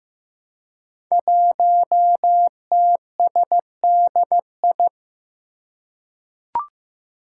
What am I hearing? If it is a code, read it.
Morse code, heard as a keyed tone, about 15 wpm, 700 Hz. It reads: 1TSDI
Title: Five Challenging Letter-Number Combinations 15wpm